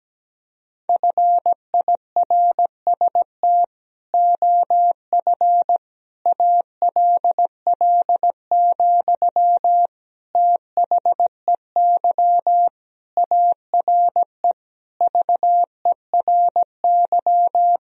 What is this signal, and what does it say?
Morse code, heard as a keyed tone, about 17 wpm, 700 Hz. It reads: FIRST OF ALL, THEY ARE VERY
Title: Sentences from Top 1000 Words - Review Part 4 - 17wpm